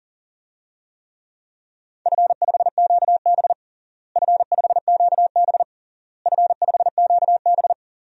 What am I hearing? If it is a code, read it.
Morse code, heard as a keyed tone, about 40 wpm, 700 Hz. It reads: F5QB F5QB F5QB
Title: DXCC-100 Call Signs - 4 Characters - Repeated 3x - 40wpm